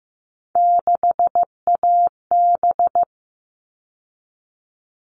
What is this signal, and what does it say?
Morse code, heard as a keyed tone, about 15 wpm, 700 Hz. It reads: 6AB
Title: All Cut Numbers - 3-Digit 15wpm